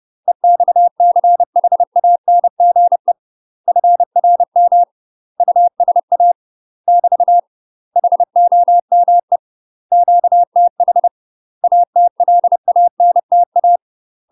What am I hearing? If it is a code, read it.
Morse code, heard as a keyed tone, about 30 wpm, 700 Hz. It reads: EXCHANGE FRM USA <BT> HOME QTH ATLANTA
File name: UK MASTER QSOs TRAINING / 30WPM / UK MASTER QSOs 0017